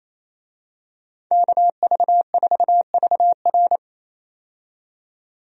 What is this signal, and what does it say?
Morse code, heard as a keyed tone, about 28 wpm, 700 Hz. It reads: KV4VR
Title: USA Call Signs - 5 Characters - 28wpm